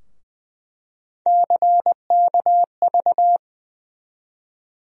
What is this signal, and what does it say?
Morse code, heard as a keyed tone, about 20 wpm, 700 Hz. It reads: CKV
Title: USA Call Sign Postfixes - 3 Letters 20wpm